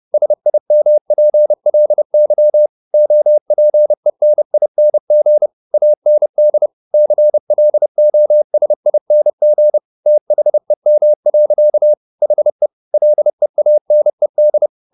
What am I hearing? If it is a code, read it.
Morse code, heard as a keyed tone, about 30 wpm, 600 Hz. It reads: SIMPLY OPENING AND CLOSING THEM. HE LEANED